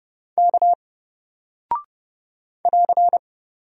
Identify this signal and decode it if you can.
Morse code, heard as a keyed tone, about 29 wpm, 700 Hz. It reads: K <AR>
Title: Single Character (Letter:Number:Punctuation:Prosign) Alt 30wpm